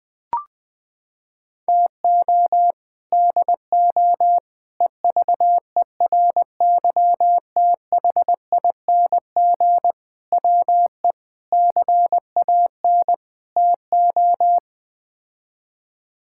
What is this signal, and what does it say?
Morse code, heard as a keyed tone, about 21 wpm, 700 Hz. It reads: TO DO EVERYTHING WE CAN TO
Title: Sets of 6 Words - 20wpm